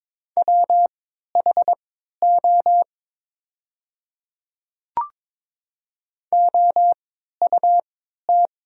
Morse code, heard as a keyed tone, about 22 wpm, spaced as for 13 wpm, 700 Hz. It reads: WHO OUT
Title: 3x Character Spacing - Top 300 Words - 22wpm